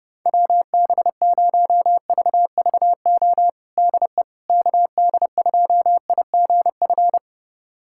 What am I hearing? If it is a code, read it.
Morse code, heard as a keyed tone, about 32 wpm, 700 Hz. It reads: WB0VVO DE KD2IGF
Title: USA Callsign DE Callsign - 6 Characters - 30wpm